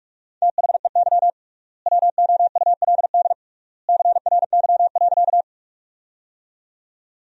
Morse code, heard as a keyed tone, about 43 wpm, 700 Hz. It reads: THEY WOULD CRY.
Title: 3x Word-Spacing - Sentences from Top 500 Words - 45wpm